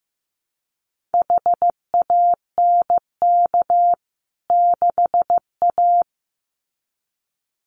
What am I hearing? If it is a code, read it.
Morse code, heard as a keyed tone, about 15 wpm, 700 Hz. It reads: HANK 6A
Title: CWT Exchange 15wpm